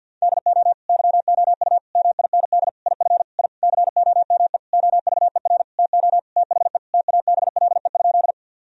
Morse code, heard as a keyed tone, about 50 wpm, 700 Hz. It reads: DO YOU MIND IF I COME OVER TO THE TABLE?
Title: Sentences from Top 1000 Words - Review Part 2 - 50wpm